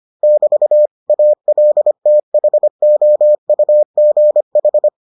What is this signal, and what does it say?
Morse code, heard as a keyed tone, about 25 wpm, 600 Hz. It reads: <BT> ALTHOUGH